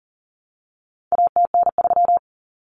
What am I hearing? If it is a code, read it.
Morse code, heard as a keyed tone, about 40 wpm, 700 Hz. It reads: ATN3
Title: Common Cut Numbers - 4-Digit 40wpm